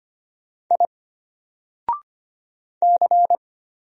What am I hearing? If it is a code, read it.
Morse code, heard as a keyed tone, about 25 wpm, 700 Hz. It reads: I C
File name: Single-Letter-Number-25wpm